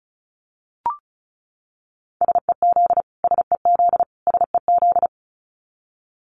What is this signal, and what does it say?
Morse code, heard as a keyed tone, about 34 wpm, 700 Hz. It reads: SEZ SEZ SEZ
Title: Abbreviations Repeated 3x - 35wpm